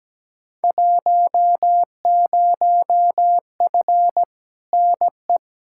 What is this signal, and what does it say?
Morse code, heard as a keyed tone, about 17 wpm, 700 Hz. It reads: 10F NE
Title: Field Day Exchanges Encore - 17wpm